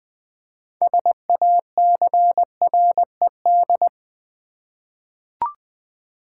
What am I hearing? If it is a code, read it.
Morse code, heard as a keyed tone, about 20 wpm, 700 Hz. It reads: SACRED
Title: 6 Letter Words - First 3 Letters Given - 20wpm